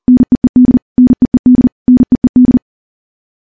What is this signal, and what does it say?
Morse code, heard as a keyed tone, about 40 wpm, 250 Hz. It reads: NEED NEED NEED
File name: morse / morse-training-40WPM